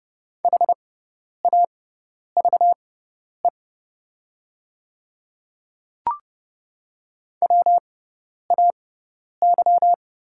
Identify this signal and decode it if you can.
Morse code, heard as a keyed tone, about 30 wpm, spaced as for 10 wpm, 700 Hz. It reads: HAVE WAY
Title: 6x Character Spacing - Top 100 Words - 30wpm